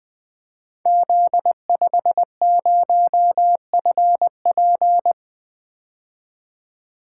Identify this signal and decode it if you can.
Morse code, heard as a keyed tone, about 20 wpm, 700 Hz. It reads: Z50FP
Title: Code Groups - 5 Letter:Numbers - 20wpm